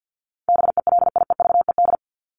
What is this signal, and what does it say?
Morse code, heard as a keyed tone, about 50 wpm, 700 Hz. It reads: BELIEVED
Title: ED Suffix - Top 100 Words with Base Words 50wpm